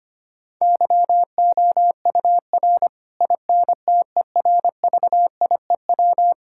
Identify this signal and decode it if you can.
Morse code, heard as a keyed tone, about 25 wpm, 700 Hz. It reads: YOUR INTERVIEW
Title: Sentences from Top 1600 Words 25wpm